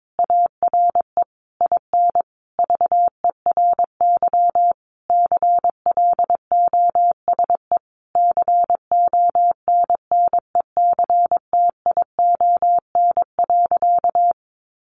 Morse code, heard as a keyed tone, about 22 wpm, 700 Hz. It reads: ARE IN VERY CLOSE CONNECTION.
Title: Sentences from Top 1300 Words 22wpm